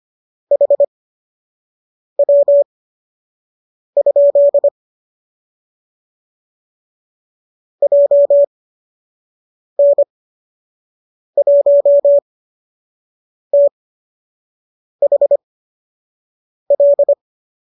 Morse code, heard as a keyed tone, about 25 wpm, spaced as for 6 wpm, 570 Hz. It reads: HW? JN1THL
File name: qso003 06